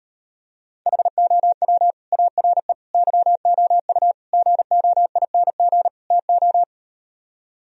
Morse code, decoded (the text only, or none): HOW ARE YOU GOING TO